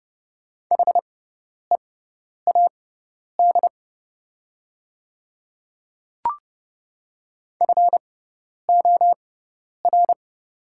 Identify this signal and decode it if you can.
Morse code, heard as a keyed tone, about 30 wpm, spaced as for 10 wpm, 700 Hz. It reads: HEAD FOR
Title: 6x Character Spacing - Top 500 Words - 30wpm